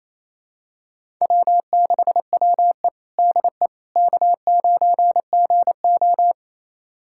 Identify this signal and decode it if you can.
Morse code, heard as a keyed tone, about 28 wpm, 700 Hz. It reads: W6WE DE K9GO
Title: USA Callsign DE Callsign - 4 Characters - 28wpm